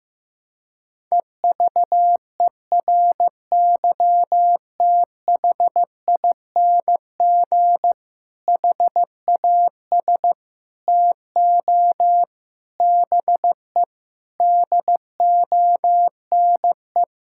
Morse code, heard as a keyed tone, about 15 wpm, 700 Hz. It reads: EVERYTHING HAS TO BE DONE
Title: Sentences from Top 1600 Words 15wpm